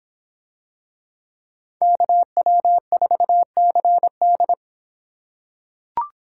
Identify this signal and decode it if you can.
Morse code, heard as a keyed tone, about 26 wpm, 700 Hz. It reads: KW4CD